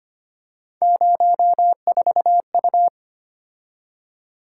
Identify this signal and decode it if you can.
Morse code, heard as a keyed tone, about 25 wpm, 700 Hz. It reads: 04U
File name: Three Letter-Number Combinations - 25wpm